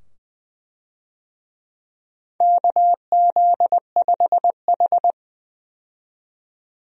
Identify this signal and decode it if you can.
Morse code, heard as a keyed tone, about 20 wpm, 700 Hz. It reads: KZ5H
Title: USA Call Signs - 4 Characters - 20wpm